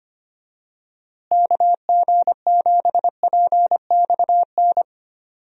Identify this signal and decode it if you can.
Morse code, heard as a keyed tone, about 25 wpm, 700 Hz. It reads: KG7PXN
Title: USA Call Signs - 6 Characters - Repeated 2x - 25wpm